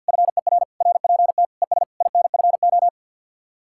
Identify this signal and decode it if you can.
Morse code, heard as a keyed tone, about 65 wpm, spaced as for 57 wpm, 700 Hz. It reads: FER RPT ES INFO
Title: Speed-Racing Common QSO Phrases Encore - 50wpm